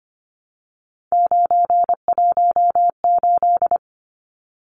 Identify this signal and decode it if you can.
Morse code, heard as a keyed tone, about 25 wpm, 700 Hz. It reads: 918
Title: Numbers - 3-Digit 25wpm